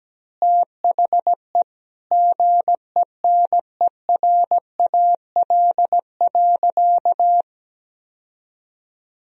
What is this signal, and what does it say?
Morse code, heard as a keyed tone, about 17 wpm, 700 Hz. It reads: THE GENERAL.
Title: Sentences from Top 1000 Words - Review Part 3 - 17wpm